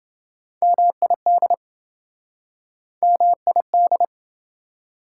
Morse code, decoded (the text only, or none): MID MID